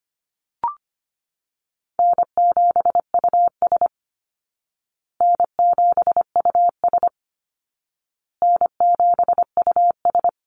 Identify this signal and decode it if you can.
Morse code, heard as a keyed tone, about 25 wpm, 700 Hz. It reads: N7US N7US N7US
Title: Call Signs Repeated 2x - 25wpm